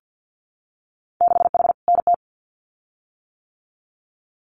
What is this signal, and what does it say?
Morse code, heard as a keyed tone, about 50 wpm, 700 Hz. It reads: 6H NT
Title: Winter Field Day Exchanges Encore 50wpm